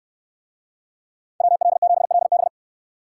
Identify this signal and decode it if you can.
Morse code, heard as a keyed tone, about 69 wpm, 700 Hz. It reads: WL7LB